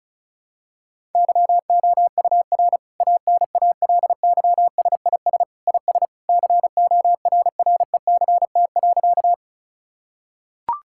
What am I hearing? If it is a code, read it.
Morse code, heard as a keyed tone, about 34 wpm, 700 Hz. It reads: YOUR ANALYSIS IS CORRECT.
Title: Sentences from Top 1000 Words - Review Part 5 - 35wpm